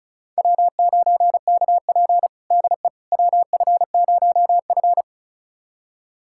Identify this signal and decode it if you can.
Morse code, heard as a keyed tone, about 36 wpm, 700 Hz. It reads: W9KP DE WF0F
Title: USA Callsign DE Callsign - 4 Characters - 35wpm